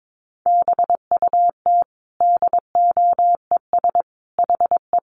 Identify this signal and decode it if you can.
Morse code, heard as a keyed tone, about 22 wpm, 700 Hz. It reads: BUT DOES HE
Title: Sentences from Top 1200 Words 22wpm